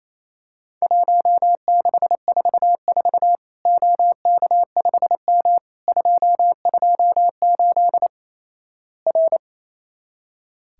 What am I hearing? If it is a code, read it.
Morse code, heard as a keyed tone, about 28 wpm, 700 Hz. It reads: 1644 OK5M 228 R
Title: Worked All Europe DX Contest - QTC Exchange - 28wpm